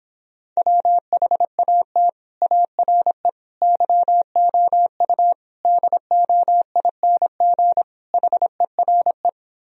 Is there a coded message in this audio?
Yes: Morse code from 0.6 to 9.3 s, heard as a keyed tone, about 26 wpm, 700 Hz. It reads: WHAT ARE YOU DOING HERE